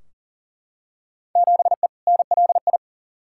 Morse code, heard as a keyed tone, about 37 wpm, 700 Hz. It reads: 7E NLI
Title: Field Day Exchanges Encore - 40wpm